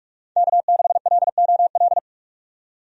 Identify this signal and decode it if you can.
Morse code, heard as a keyed tone, about 46 wpm, 700 Hz. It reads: K6LOL
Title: USA Call Signs - 5 Characters - 45wpm